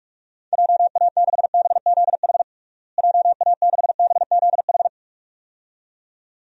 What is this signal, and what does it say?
Morse code, heard as a keyed tone, about 46 wpm, 700 Hz. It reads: JA6BZH JA6BZH
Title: DXCC-100 Call Signs - 6 Characters - Repeated 2x - 45wpm